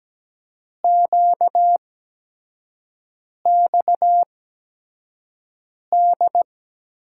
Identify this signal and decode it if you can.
Morse code, heard as a keyed tone, about 17 wpm, spaced as for 5 wpm, 700 Hz. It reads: QXD
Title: USA Call Signs - 8x Character Spacing - 17wpm